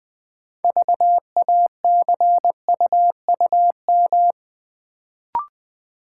Morse code, heard as a keyed tone, about 20 wpm, 690 Hz. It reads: VACUUM
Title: Build A Word - 2 Letters At A Time - 6-Letter Words - 20wpm